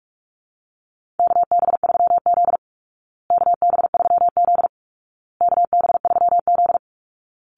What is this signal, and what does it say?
Morse code, heard as a keyed tone, about 44 wpm, 700 Hz. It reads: KB3Z KB3Z KB3Z
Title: Call Signs Repeated 2x - 45wpm